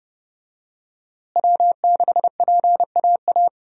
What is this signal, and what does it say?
Morse code, heard as a keyed tone, about 30 wpm, 700 Hz. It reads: W6PAA
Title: USA Call Signs - 5 Characters - Repeated 2x - 30wpm